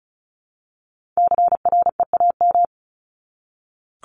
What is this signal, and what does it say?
Morse code, heard as a keyed tone, about 34 wpm, 700 Hz. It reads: CREAM
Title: Top 1251-1500 Words 35wpm